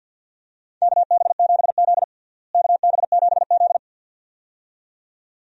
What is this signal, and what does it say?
Morse code, heard as a keyed tone, about 50 wpm, 700 Hz. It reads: KB7Z KB7Z
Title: USA Call Signs - 4 Characters - Repeated 2x - 50wpm